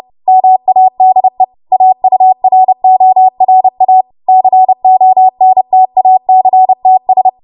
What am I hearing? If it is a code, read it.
Morse code, heard as a keyed tone, about 30 wpm, 750 Hz. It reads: MADE AURORA CONTACTS
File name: Morse / 30WPM